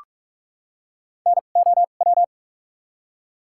Morse code, heard as a keyed tone, about 45 wpm, spaced as for 32 wpm, 700 Hz. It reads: NOW
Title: 2x Character Spacing - Top 300 Words - 45wpm